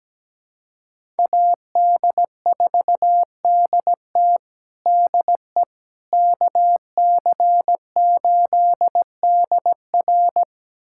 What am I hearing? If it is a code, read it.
Morse code, heard as a keyed tone, about 17 wpm, 700 Hz. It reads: AD4DT DE KC8DR
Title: USA Callsign DE Callsign - 5 Characters - 17wpm